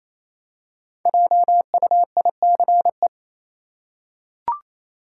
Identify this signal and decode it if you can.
Morse code, heard as a keyed tone, about 28 wpm, 700 Hz. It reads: JUICE
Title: Five Letter Words - 28wpm